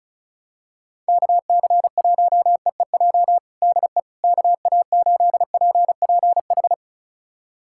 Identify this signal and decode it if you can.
Morse code, heard as a keyed tone, about 36 wpm, 700 Hz. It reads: KC1EEJ DE KA8PPH
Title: USA Callsign DE Callsign - 6 Characters - 35wpm